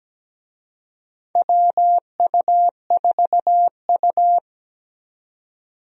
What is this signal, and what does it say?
Morse code, heard as a keyed tone, about 17 wpm, 700 Hz. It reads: WU4U
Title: USA Call Signs - 4 Characters - 17wpm